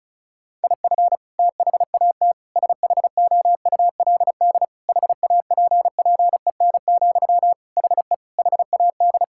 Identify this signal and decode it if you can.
Morse code, heard as a keyed tone, about 36 wpm, 700 Hz. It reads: IF THAT SHOULD HAPPEN, HE HAD
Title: Sentences from Top 1000 Words - Review Part 6 - 35wpm